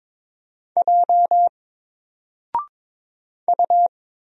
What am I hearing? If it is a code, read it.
Morse code, heard as a keyed tone, about 22 wpm, 700 Hz. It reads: J U
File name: Single-Letter-Number-22wpm